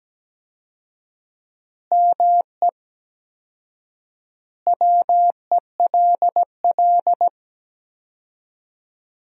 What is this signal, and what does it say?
Morse code, heard as a keyed tone, about 17 wpm, 700 Hz. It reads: ME WELL
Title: 4x Word-Spacing - Sentences from Top 400 Words - 17wpm